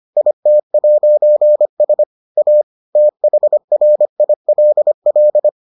Morse code, heard as a keyed tone, about 25 wpm, 600 Hz. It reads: IT'S A THRILL